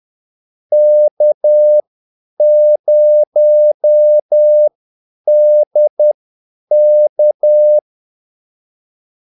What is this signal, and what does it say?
Morse code, heard as a keyed tone, about 10 wpm, spaced as for 8 wpm, 600 Hz. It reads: K0DK
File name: Callsigns-10-8wpm